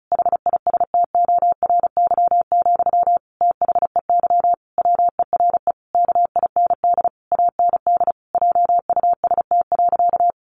Morse code, heard as a keyed tone, about 34 wpm, 700 Hz. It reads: HISTORY, THEY WERE KIND AND JUST.
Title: Sentences from Top 800 Words 35wpm